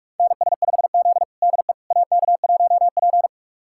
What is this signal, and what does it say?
Morse code, heard as a keyed tone, about 41 wpm, 700 Hz. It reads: NS5Z DE AK1P
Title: USA Callsign DE Callsign - 4 Characters - 45wpm